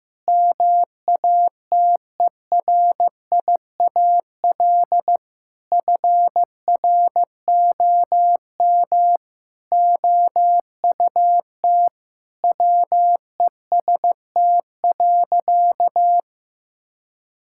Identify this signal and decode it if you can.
Morse code, heard as a keyed tone, about 15 wpm, 700 Hz. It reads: MATERIAL FROM OUT WEST.
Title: Sentences from Top 1000 Words - Review Part 5 - 15wpm